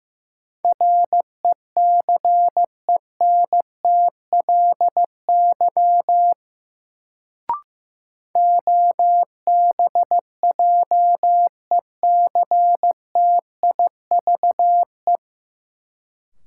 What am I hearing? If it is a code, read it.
Morse code, heard as a keyed tone, about 15 wpm, 700 Hz. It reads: RECENTLY OBJECTIVE
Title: Sentences from Top 1200 Words - New Word Drill - 15wpm